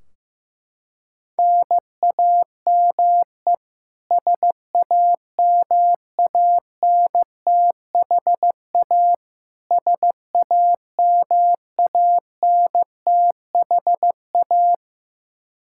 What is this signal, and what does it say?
Morse code, heard as a keyed tone, about 15 wpm, 700 Hz. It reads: NAME SAMANTHA SAMANTHA
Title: Practice QSO - Exchange 1 - Beginning - 15wpm